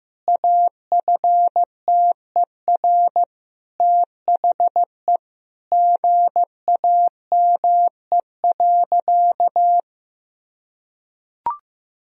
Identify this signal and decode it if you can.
Morse code, heard as a keyed tone, about 14 wpm, 700 Hz. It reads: AFTER THE GAME.
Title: Sentences from Top 1000 Words - Review Part 4 - 15wpm